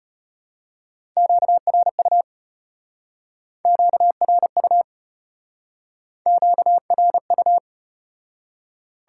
Morse code, heard as a keyed tone, about 34 wpm, 700 Hz. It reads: QRU QRU QRU